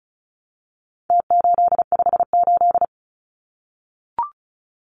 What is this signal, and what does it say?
Morse code, heard as a keyed tone, about 34 wpm, 700 Hz. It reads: T858
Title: Numbers - Contest Style - 4-Digit 35wpm